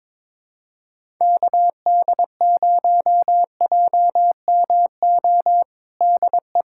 Morse code, heard as a keyed tone, about 22 wpm, 700 Hz. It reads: KD0JMO DE
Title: USA Callsign DE Callsign - 6 Characters - 22wpm